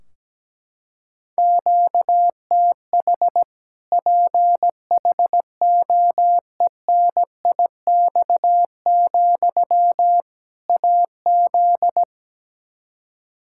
Morse code, heard as a keyed tone, about 17 wpm, 700 Hz. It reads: QTH PHOENIX, AZ
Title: Practice QSO - Exchange 1 - Beginning - 17wpm